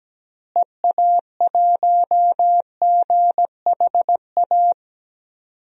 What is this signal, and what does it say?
Morse code, heard as a keyed tone, about 17 wpm, 700 Hz. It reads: EA1GHA
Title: DXCC-100 Call Signs - 6 Characters - Repeated 3x - 17wpm